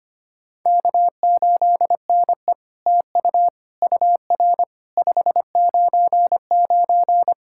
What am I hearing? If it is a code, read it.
Morse code, heard as a keyed tone, about 25 wpm, 700 Hz. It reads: K8NE TU UR 599
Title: SOTA QSO Practice 25wpm